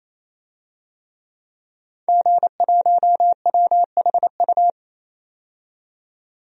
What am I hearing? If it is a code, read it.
Morse code, heard as a keyed tone, about 28 wpm, 690 Hz. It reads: G1WHU
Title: DXCC-100 Call Signs - 5 Characters - 28wpm